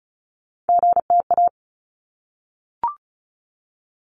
Morse code, heard as a keyed tone, about 34 wpm, 700 Hz. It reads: GTA
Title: ARRL Sections 35wpm